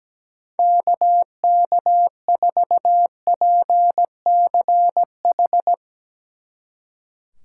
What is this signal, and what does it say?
Morse code, heard as a keyed tone, about 19 wpm, 700 Hz. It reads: KK4PCH
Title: Speed Racing - USA 2x3 Call Signs 17wpm